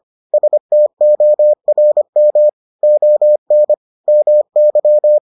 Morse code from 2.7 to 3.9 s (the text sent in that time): ON